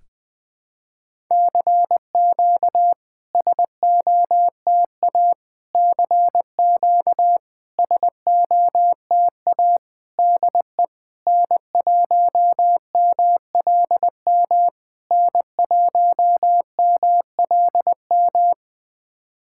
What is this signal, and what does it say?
Morse code, heard as a keyed tone, about 20 wpm, 700 Hz. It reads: CQ SOTA CQ SOTA DE N1MLM N1MLM
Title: SOTA QSO Practice 20wpm